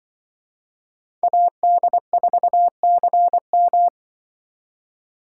Morse code, heard as a keyed tone, about 25 wpm, 700 Hz. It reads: AD4CM
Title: Speed Racing - USA 2x2 Call Signs 20wpm